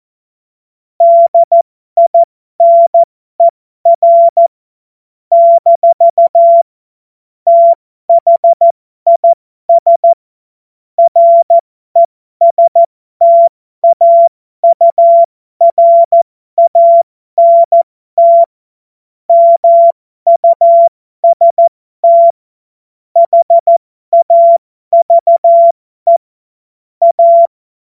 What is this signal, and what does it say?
Morse code, heard as a keyed tone, about 14 wpm, spaced as for 12 wpm, 680 Hz. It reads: DINER - THIS RESTAURANT MUST HAVE A
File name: Morsekurs / Witze